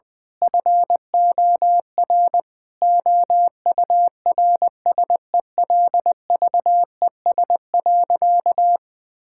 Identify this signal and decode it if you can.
Morse code, heard as a keyed tone, about 20 wpm, 700 Hz. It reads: FOR OURSELVES.
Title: Sentences from Top 1600 Words 20wpm